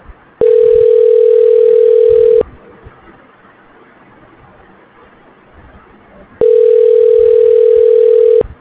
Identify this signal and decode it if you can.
Ringback tone (North America), 440 + 480 Hz.